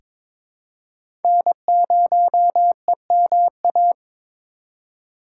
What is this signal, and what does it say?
Morse code, heard as a keyed tone, about 22 wpm, 700 Hz. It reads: N0EMA
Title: USA Call Signs - 5 Characters - 2x Character Spacing - 22wpm